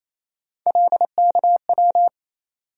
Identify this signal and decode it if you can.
Morse code, heard as a keyed tone, about 28 wpm, 700 Hz. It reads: LKW
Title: USA Call Sign Postfixes - 3 Letters 28wpm